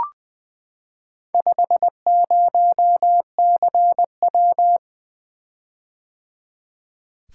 Morse code, heard as a keyed tone, about 20 wpm, 700 Hz. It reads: 50CW
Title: Four Letter-Number Combinations - 20wpm